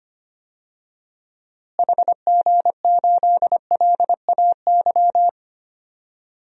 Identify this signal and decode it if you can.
Morse code, heard as a keyed tone, about 25 wpm, 700 Hz. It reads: HG8LAY
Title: DXCC-100 Call Signs - 6 Characters - Repeated 2x - 25wpm